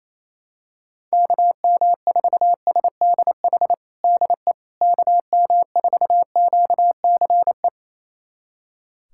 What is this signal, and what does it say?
Morse code, heard as a keyed tone, about 28 wpm, 700 Hz. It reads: KM4SDH DE KM4QCE
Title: USA Callsign DE Callsign 28wpm